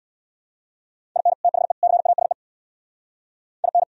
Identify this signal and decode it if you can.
Morse code, heard as a keyed tone, about 76 wpm, 700 Hz. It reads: IN THE CASE IN